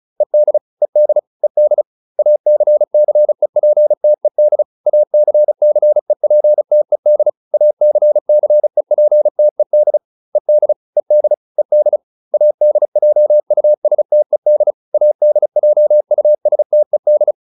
Morse code, heard as a keyed tone, about 35 wpm, 600 Hz. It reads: ED ED ED ACCEPTED ACCEPTED ACCEPTED ED ED ED ADJUSTED ADJUSTED